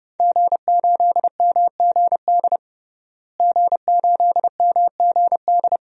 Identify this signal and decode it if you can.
Morse code, heard as a keyed tone, about 30 wpm, 700 Hz. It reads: G8MGD G8MGD
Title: DXCC-100 Call Signs - 5 Characters - Repeated 2x - 30wpm